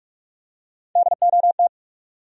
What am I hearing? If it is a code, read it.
Morse code, heard as a keyed tone, about 47 wpm, 700 Hz. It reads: DOT
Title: USA Common Names - Max 5 Letters - 45wpm